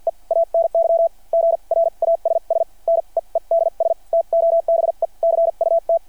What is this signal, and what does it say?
Morse code, heard as a keyed tone, about 53 wpm, spaced as for 40 wpm, 690 Hz. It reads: E ANY GRASS NEEDS TO BE CUT
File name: archived / CRSnet